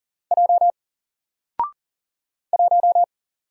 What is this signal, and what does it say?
Morse code, heard as a keyed tone, about 39 wpm, 700 Hz. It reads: J 1
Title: Single Letter-Number 40wpm